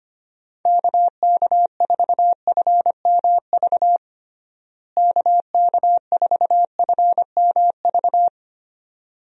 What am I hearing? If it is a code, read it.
Morse code, heard as a keyed tone, about 25 wpm, 700 Hz. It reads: KK4FMV KK4FMV
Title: USA Call Signs - 6 Characters - Repeated 2x - 25wpm